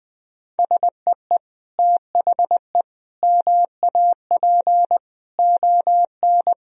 Morse code, heard as a keyed tone, about 20 wpm, 700 Hz. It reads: SEE THE MAP ON
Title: Sentences from Top 1000 Words - Review Part 2 - 20wpm